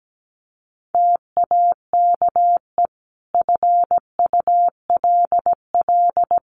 Morse code, heard as a keyed tone, about 17 wpm, 700 Hz. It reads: TAKE FULL
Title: Sentences from Top 1400 Words 17wpm